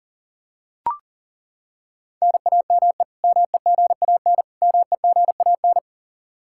Morse code, heard as a keyed tone, about 37 wpm, 700 Hz. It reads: NAME MEGAN MEGAN
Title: Practice QSO - Exchange 1 - Beginning - 40wpm